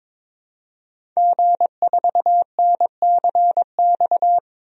Morse code, heard as a keyed tone, about 21 wpm, 700 Hz. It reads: G4NCX
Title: DXCC-100 Call Signs - 5 Characters - Repeated 2x - 22wpm